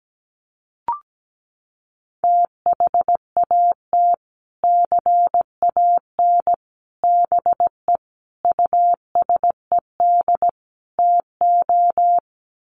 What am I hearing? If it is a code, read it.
Morse code, heard as a keyed tone, about 17 wpm, 700 Hz. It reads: THAT CAN BE USED TO
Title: Sets of 5 Words Encore 17wpm